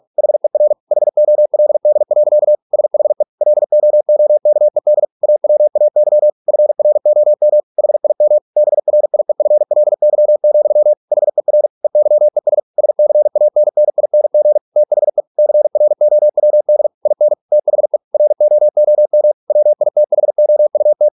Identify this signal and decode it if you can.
Morse code, heard as a keyed tone, about 46 wpm, 600 Hz. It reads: HER HOLD. SHE LOOKED AWAY FROM HIM BRIEFLY, HER EYES SCANNING THE CROWD IN THE ROOM WITHOUT